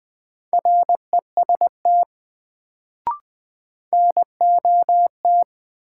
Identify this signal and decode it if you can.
Morse code, heard as a keyed tone, about 20 wpm, 700 Hz. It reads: REST NOT
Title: Sets of 2 Words - New Word Drill - 20wpm